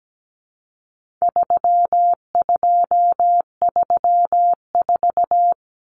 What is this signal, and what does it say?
Morse code, heard as a keyed tone, about 17 wpm, 700 Hz. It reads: 3234
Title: Common Cut Numbers - 4-Digit 17wpm